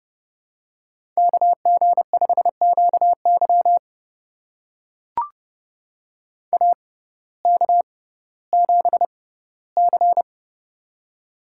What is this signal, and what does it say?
Morse code, heard as a keyed tone, about 29 wpm, spaced as for 16 wpm, 700 Hz. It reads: KG5QY AK7C
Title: USA Call Signs - 6x Character Spacing - 30wpm